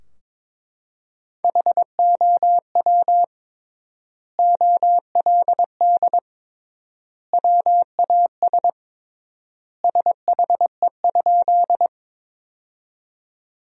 Morse code, heard as a keyed tone, about 22 wpm, 700 Hz. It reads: HOW OLD WAS SHE?
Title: 3x Word-Spacing - Sentences from Top 200 Words - 22wpm